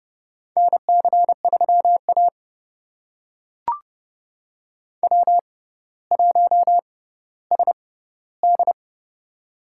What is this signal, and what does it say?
Morse code, heard as a keyed tone, about 28 wpm, spaced as for 15 wpm, 700 Hz. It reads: NC3A W1SD
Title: USA Call Signs - 4 Characters - 6x Character Spacing - 30wpm